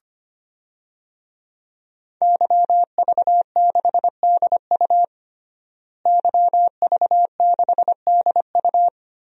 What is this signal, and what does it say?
Morse code, heard as a keyed tone, about 25 wpm, 700 Hz. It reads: YV6DU YV6DU
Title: DXCC-100 Call Signs - 5 Characters - Repeated 3x - 25wpm